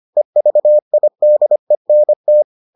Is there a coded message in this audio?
Yes: Morse code from 0.2 to 2.4 s, heard as a keyed tone, about 25 wpm, 600 Hz. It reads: EVIDENT